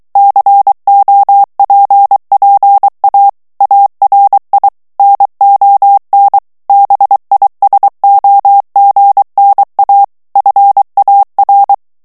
Morse code, heard as a keyed tone, about 23 wpm, 800 Hz. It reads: COPPA ARI NON BISOGNA FAR
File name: ITALIEN texte clair 2 vitesse 22 mots minute